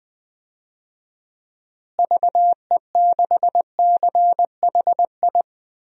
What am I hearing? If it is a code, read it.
Morse code, heard as a keyed tone, about 20 wpm, 700 Hz. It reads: VE6CHI
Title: DXCC-100 Call Signs - 6 Characters - Repeated 2x - 20wpm